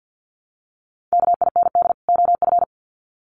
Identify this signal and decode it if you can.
Morse code, heard as a keyed tone, about 50 wpm, 700 Hz. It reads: KIND OF
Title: Sets of 2 Words 50wpm